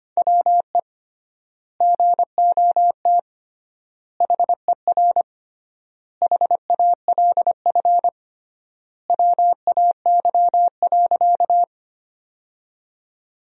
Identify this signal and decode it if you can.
Morse code, heard as a keyed tone, about 25 wpm, 700 Hz. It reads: WE GOT HER HALF WAY.
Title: 3x Word-Spacing - Sentences from Top 400 Words - 25wpm